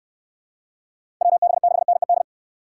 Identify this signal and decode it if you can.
Morse code, heard as a keyed tone, about 64 wpm, 700 Hz. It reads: WB6NED